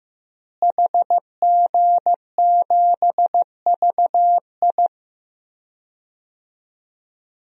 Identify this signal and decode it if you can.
Morse code, heard as a keyed tone, about 15 wpm, 700 Hz. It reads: HG7VI
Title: DXCC-100 Call Signs - 5 Characters - Repeated 3x - 15wpm